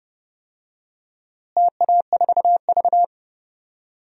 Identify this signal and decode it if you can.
Morse code, heard as a keyed tone, about 29 wpm, 700 Hz. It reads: TA4V
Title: DXCC-100 Call Signs - 4 Characters - Repeated 3x - 30wpm